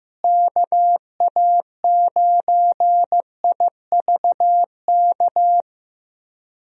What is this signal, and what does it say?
Morse code, heard as a keyed tone, about 15 wpm, 700 Hz. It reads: KA9IVK
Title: USA Call Signs - 6 Characters - Repeated 2x - 15wpm